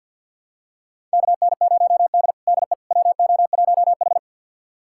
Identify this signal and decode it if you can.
Morse code, heard as a keyed tone, about 50 wpm, 700 Hz. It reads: KN0D DE WO1H
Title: USA Callsign DE Callsign - 4 Characters - 50wpm